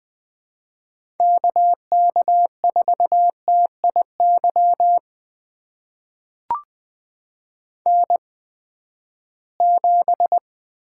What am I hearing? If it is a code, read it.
Morse code, heard as a keyed tone, about 20 wpm, spaced as for 14 wpm, 700 Hz. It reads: KK4TIY N7
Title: USA Call Signs - 8x Character Spacing - 20wpm